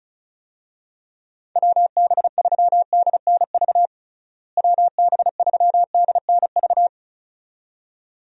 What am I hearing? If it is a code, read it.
Morse code, heard as a keyed tone, about 36 wpm, 700 Hz. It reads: WB3DNV WB3DNV
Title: USA Call Signs - 6 Characters - Repeated 2x - 35wpm